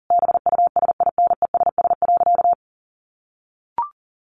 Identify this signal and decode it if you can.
Morse code, heard as a keyed tone, about 40 wpm, 700 Hz. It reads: BUSINESS.
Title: A Sound Of Thunder 40wpm